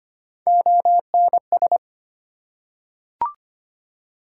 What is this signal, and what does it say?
Morse code, heard as a keyed tone, about 25 wpm, 700 Hz. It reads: ONS
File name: practice / 25wpm / ARRL Sections v2 - 25wpm